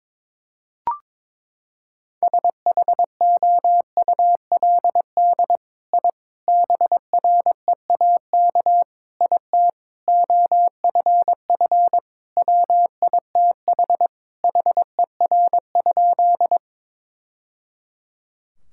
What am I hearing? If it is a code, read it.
Morse code, heard as a keyed tone, about 22 wpm, 700 Hz. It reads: SHOULD I BREAK IT OFF WITH HER?
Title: Sentences from Top 1000 Words - Review Part 6 - 22wpm